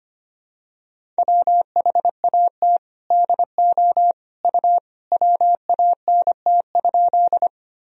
Morse code, heard as a keyed tone, about 25 wpm, 700 Hz. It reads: WHAT DO U WANT?
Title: Sentences from Top 200 Words and Abbreviations 25wpm